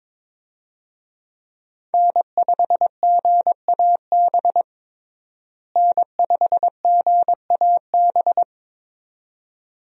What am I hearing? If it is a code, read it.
Morse code, heard as a keyed tone, about 22 wpm, 700 Hz. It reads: N5GAB N5GAB
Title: USA Call Signs - 5 Characters - Repeated 2x - 22wpm